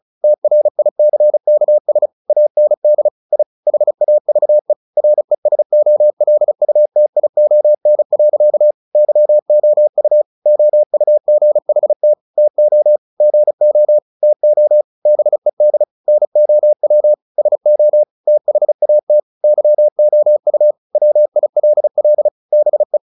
Morse code, heard as a keyed tone, about 34 wpm, 600 Hz. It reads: TRICKS AND I HAVE RESOLUTION. YOU OUGHT TO GO TO BED NOW SO THAT YOU WILL BE
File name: the old man and the sea / the old man and the sea 1 35wpm